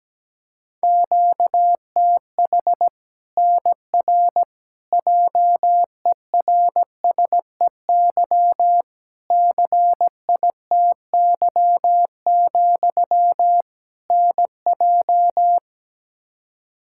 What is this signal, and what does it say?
Morse code, heard as a keyed tone, about 17 wpm, 700 Hz. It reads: QTH NR JERSEY CITY, NJ
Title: K1USN SST QSO Practice 17wpm